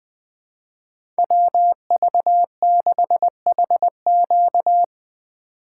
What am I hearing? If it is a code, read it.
Morse code, heard as a keyed tone, about 20 wpm, 700 Hz. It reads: WV6HQ
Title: USA Call Signs - 5 Characters - 6x Character Spacing - 20wpm